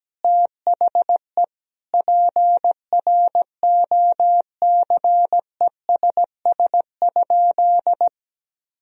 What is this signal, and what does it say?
Morse code, heard as a keyed tone, about 17 wpm, 700 Hz. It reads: THE PROCESS?
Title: Sentences from Top 1000 Words - Review Part 3 - 17wpm